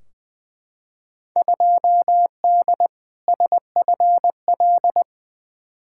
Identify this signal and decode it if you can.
Morse code, heard as a keyed tone, about 20 wpm, 700 Hz. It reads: 2D SFL
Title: Field Day Exchanges Encore - 20wpm